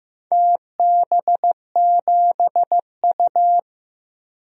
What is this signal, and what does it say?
Morse code, heard as a keyed tone, about 15 wpm, 700 Hz. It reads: TB7U
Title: DXCC-100 Call Signs - 4 Characters - 15wpm